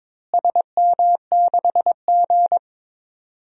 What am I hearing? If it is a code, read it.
Morse code, heard as a keyed tone, about 22 wpm, 700 Hz. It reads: SM6G
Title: DXCC-100 Call Signs - 4 Characters - Repeated 3x - 22wpm